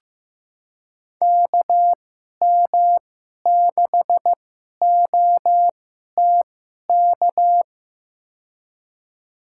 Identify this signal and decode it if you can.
Morse code, heard as a keyed tone, about 15 wpm, spaced as for 11 wpm, 700 Hz. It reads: KM6OTK
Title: USA Call Signs - 2x Character Spacing - 15wpm